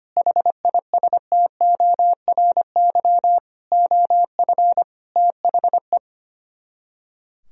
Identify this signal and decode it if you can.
Morse code, heard as a keyed tone, about 24 wpm, 700 Hz. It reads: HISTORY OF THE
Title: Speed-Racing - Sets of 4 Words 25wpm